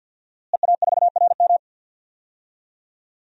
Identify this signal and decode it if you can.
Morse code, heard as a keyed tone, about 52 wpm, 700 Hz. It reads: EA4RM
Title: DXCC-100 Call Signs - 5 Characters - Repeated 3x - 50wpm